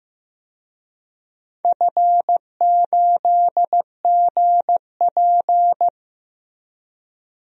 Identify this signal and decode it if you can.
Morse code, heard as a keyed tone, about 15 wpm, 700 Hz. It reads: F8GP
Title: DXCC-100 Call Signs - 4 Characters - Repeated 2x - 15wpm